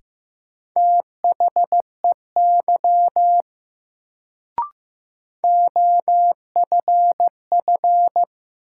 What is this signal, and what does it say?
Morse code, heard as a keyed tone, about 15 wpm, 700 Hz. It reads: THEY OFF
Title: Sets of 2 Words - New Word Drill - 15wpm